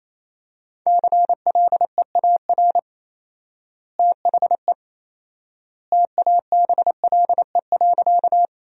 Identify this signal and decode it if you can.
Morse code, heard as a keyed tone, about 28 wpm, 700 Hz. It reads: CLEAR THE TABLE.
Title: 4x Word-Spacing - Sentences from Top 500 Words - 28wpm